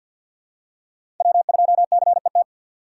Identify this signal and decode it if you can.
Morse code, heard as a keyed tone, about 52 wpm, 700 Hz. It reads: W2CET
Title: USA Call Signs - 5 Characters - 2x Character Spacing - 50wpm